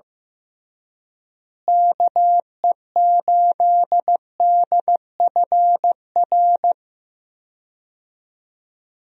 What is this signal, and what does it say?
Morse code, heard as a keyed tone, about 15 wpm, 700 Hz. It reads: KE8DFR
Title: USA Call Signs - 6 Characters - Repeated 3x - 15wpm